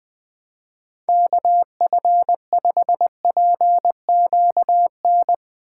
Morse code, heard as a keyed tone, about 20 wpm, 700 Hz. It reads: KF5PQN
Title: USA Callsign DE Callsign - 6 Characters - 20wpm